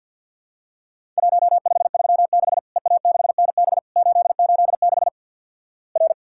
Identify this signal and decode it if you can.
Morse code, heard as a keyed tone, about 50 wpm, 700 Hz. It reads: 1536 EA6NB 886 R
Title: Worked All Europe DX Contest - QTC Exchange with Repeats - 50wpm